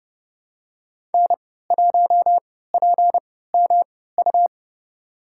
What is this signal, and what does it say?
Morse code, heard as a keyed tone, about 32 wpm, spaced as for 17 wpm, 700 Hz. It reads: N1PMU
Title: USA Call Signs - 5 Characters - 3x Character Spacing - 30wpm